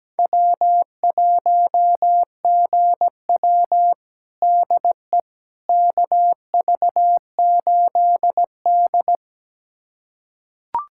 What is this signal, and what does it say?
Morse code, heard as a keyed tone, about 17 wpm, 700 Hz. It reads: W1GW DE KV8D
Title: USA Callsign DE Callsign - 4 Characters - 17wpm